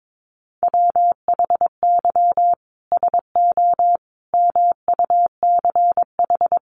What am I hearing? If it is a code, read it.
Morse code, heard as a keyed tone, about 22 wpm, 700 Hz. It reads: WHY SO MUCH
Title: Sentences from Top 1200 Words 22wpm